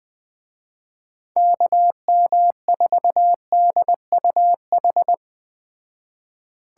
Morse code, heard as a keyed tone, about 20 wpm, 700 Hz. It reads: KM4DUH